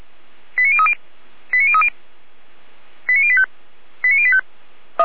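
Five-tone selective call (ZVEI), heard as quick runs of five tones, tones of about 70 ms. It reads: 89030 89030 89085 89085
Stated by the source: ZVEI five-tone call received on an SDR.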